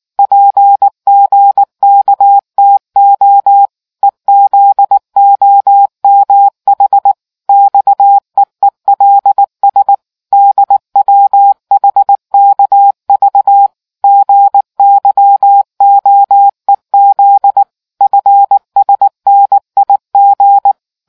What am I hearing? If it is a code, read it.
Morse code, heard as a keyed tone, about 19 wpm, 800 Hz. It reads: PGKTO EZOMH XEELS DWHKV GYOEZ FSNIG